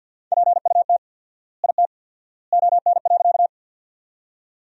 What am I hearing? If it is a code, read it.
Morse code, heard as a keyed tone, about 48 wpm, 700 Hz. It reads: PUT IT ON.
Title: 4x Word-Spacing - Sentences from Top 200 Words - 50wpm